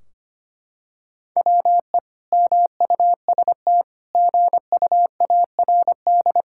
Morse code, heard as a keyed tone, about 25 wpm, 700 Hz. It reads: WE MUST GUARD
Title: Sentences from Top 1600 Words 25wpm